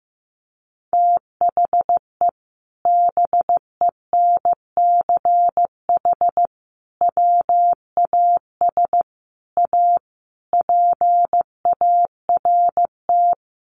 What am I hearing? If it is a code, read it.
Morse code, heard as a keyed tone, about 15 wpm, 700 Hz. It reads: THE BENCH WAS A PART
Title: Sentences from Top 1400 Words 15wpm